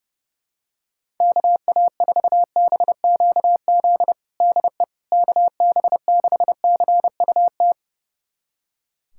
Morse code, heard as a keyed tone, about 30 wpm, 700 Hz. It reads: KA4BQZ DE KB6CUT
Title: USA Callsign DE Callsign - 6 Characters - 30wpm